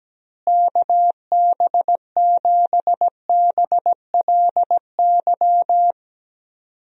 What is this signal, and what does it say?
Morse code, heard as a keyed tone, about 17 wpm, 700 Hz. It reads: KB7BLY
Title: USA Call Signs - 6 Characters - Repeated 3x - 17wpm